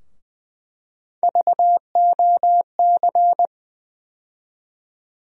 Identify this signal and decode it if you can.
Morse code, heard as a keyed tone, about 20 wpm, 700 Hz. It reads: VOC
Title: USA Call Sign Postfixes - 3 Letters 20wpm